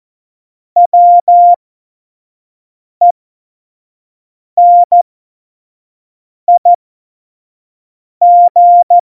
Morse code, heard as a keyed tone, about 14 wpm, spaced as for 5 wpm, 700 Hz. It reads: WENIG
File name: Morsekurs / Witze / Witz 02. Juli 2019, 25 F70